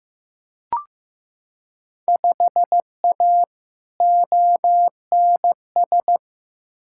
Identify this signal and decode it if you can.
Morse code, heard as a keyed tone, about 15 wpm, 700 Hz. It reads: 5A ONS
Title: Field Day Exchanges Encore - 15wpm